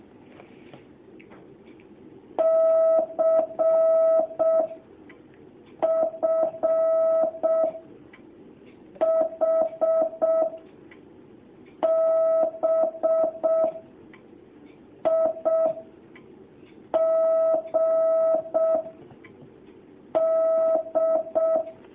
Morse code, heard as a keyed tone, about 6 wpm, 650 Hz. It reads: C F H B I G D
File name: jh8cbh / 2019cw